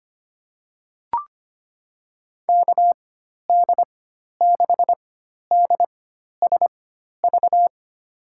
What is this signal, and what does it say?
Morse code, heard as a keyed tone, about 25 wpm, spaced as for 12 wpm, 700 Hz. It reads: KD6DSV
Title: USA Call Signs - 6 Characters - 4x Character Spacing - 25wpm